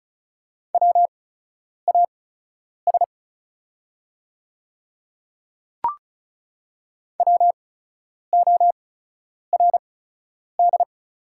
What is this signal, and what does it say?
Morse code, heard as a keyed tone, about 36 wpm, spaced as for 10 wpm, 700 Hz. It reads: WAS WORD
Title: 8x Character Spacing - Top 100 Words - 35wpm